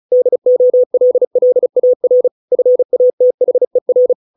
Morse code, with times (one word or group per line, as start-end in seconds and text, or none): 0.1-2.3 DOLLAR
2.5-4.1 FATHER